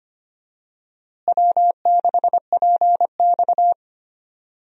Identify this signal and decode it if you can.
Morse code, heard as a keyed tone, about 25 wpm, 700 Hz. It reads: W6PX
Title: USA Call Signs - 4 Characters - Repeated 3x - 25wpm